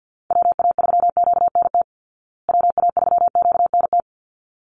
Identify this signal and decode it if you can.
Morse code, heard as a keyed tone, about 53 wpm, 700 Hz. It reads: WA3QNT WA3QNT
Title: Call Signs Repeated 3x - 50wpm